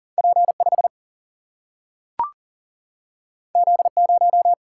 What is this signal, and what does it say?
Morse code, heard as a keyed tone, about 42 wpm, 700 Hz. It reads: P5 Z0
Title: Single-Letter Single-Number 40wpm